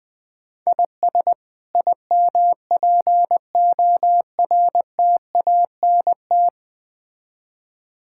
Morse code, heard as a keyed tone, about 20 wpm, 700 Hz. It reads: IS IMPORTANT
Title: Build A Phrase - 5-Word Phrases - 20wpm